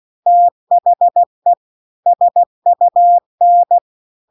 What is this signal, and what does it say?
Morse code, heard as a keyed tone, about 16 wpm, 700 Hz. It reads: THE SUN
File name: 16WPM